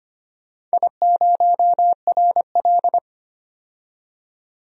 Morse code, heard as a keyed tone, about 25 wpm, 700 Hz. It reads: I0RL
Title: DXCC-100 Call Signs - 4 Characters - 25wpm